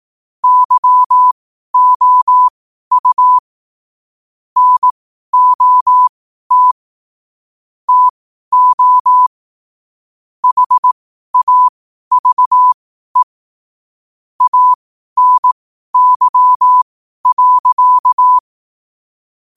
Morse code, heard as a keyed tone, about 18 wpm, spaced as for 12 wpm, 1000 Hz. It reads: YOU NOT TO HAVE ANY.
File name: qotdfiles / 12wpm